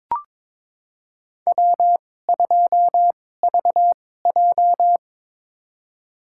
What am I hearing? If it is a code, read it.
Morse code, heard as a keyed tone, about 22 wpm, spaced as for 16 wpm, 700 Hz. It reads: W2VJ
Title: USA Call Signs - 4 Characters - 2x Character Spacing - 22wpm